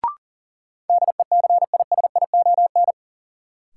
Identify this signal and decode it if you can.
Morse code, heard as a keyed tone, about 40 wpm, 700 Hz. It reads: DECISION
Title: Sentences from Top 1000 Words - New Word Drill - 40wpm